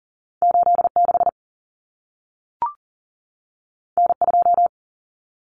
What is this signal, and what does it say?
Morse code, heard as a keyed tone, about 40 wpm, 700 Hz. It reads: 86 N2
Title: Common Cut Numbers - 2-Digit 40wpm